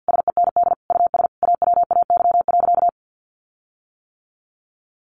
Morse code, heard as a keyed tone, about 50 wpm, 700 Hz. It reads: SEND US AWAY.
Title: Sentences from Top 1500 Words 50wpm